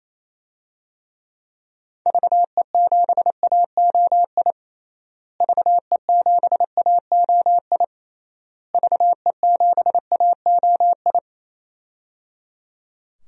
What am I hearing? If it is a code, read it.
Morse code, heard as a keyed tone, about 28 wpm, 700 Hz. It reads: VE7AOI VE7AOI VE7AOI
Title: DXCC-100 Call Signs - 6 Characters - Repeated 3x - 28wpm